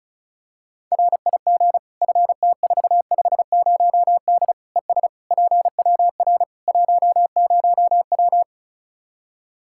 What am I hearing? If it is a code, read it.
Morse code, heard as a keyed tone, about 36 wpm, 700 Hz. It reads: RIG FT450D ES PWR 10W
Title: Practice QSO - Exchange 2 - Info - 35wpm